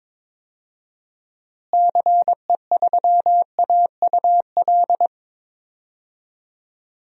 Morse code, heard as a keyed tone, about 22 wpm, 700 Hz. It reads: CE3AUL
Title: DXCC-100 Call Signs - 6 Characters - Repeated 2x - 22wpm